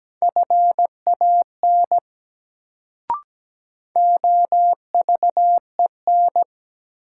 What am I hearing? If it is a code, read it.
Morse code, heard as a keyed tone, about 17 wpm, 700 Hz. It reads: FAN OVEN
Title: Sentences from Top 1000 Words - Review Part 4 - New Word Drill - 17wpm